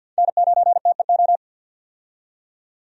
Morse code, heard as a keyed tone, about 52 wpm, 700 Hz. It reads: N9TEO
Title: USA Call Signs - 5 Characters - 50wpm